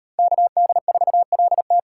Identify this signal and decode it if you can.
Morse code, heard as a keyed tone, about 34 wpm, 700 Hz. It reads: KD4LT